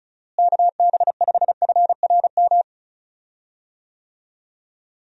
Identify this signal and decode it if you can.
Morse code, heard as a keyed tone, about 34 wpm, 700 Hz. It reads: KB5FRM
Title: USA Call Signs - 6 Characters - Repeated 2x - 35wpm